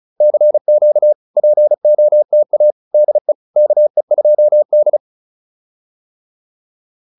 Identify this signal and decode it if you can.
Morse code, heard as a keyed tone, about 36 wpm, 600 Hz. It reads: CQ POTA DE KE2D